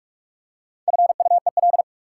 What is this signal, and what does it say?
Morse code, heard as a keyed tone, about 44 wpm, 700 Hz. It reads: FUEL
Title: Four Letter Words - 45wpm